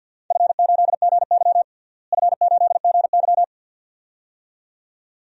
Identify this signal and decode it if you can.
Morse code, heard as a keyed tone, about 49 wpm, 700 Hz. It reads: F8GY F8GY
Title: DXCC-100 Call Signs - 4 Characters - Repeated 2x - 50wpm